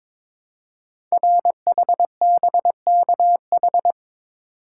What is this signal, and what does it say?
Morse code, heard as a keyed tone, about 23 wpm, 700 Hz. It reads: RHBKH
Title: Code Groups - 5 Letters - 22wpm